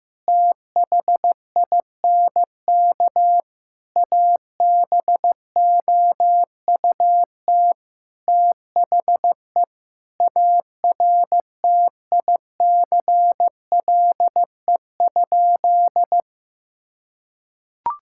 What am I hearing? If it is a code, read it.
Morse code, heard as a keyed tone, about 15 wpm, 700 Hz. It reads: THINK ABOUT THE ARTICLE?
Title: Sentences from Top 1000 Words - Review Part 5 - 15wpm